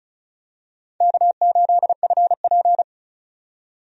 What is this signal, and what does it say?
Morse code, heard as a keyed tone, about 36 wpm, 700 Hz. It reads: K8FP
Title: USA Call Signs - 4 Characters - 35wpm